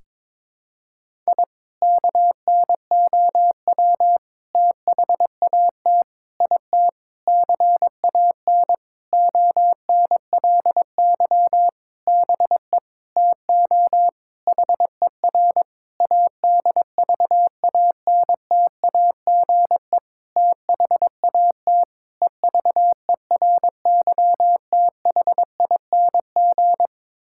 Morse code, heard as a keyed tone, about 22 wpm, 700 Hz. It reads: I KNOW THAT IT CAN ONLY BE TO HER ADVANTAGE THAT EVERYTHING
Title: Sentences from Top 1000 Words - Review Part 6 - 22wpm